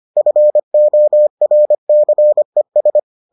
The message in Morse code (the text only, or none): FORCES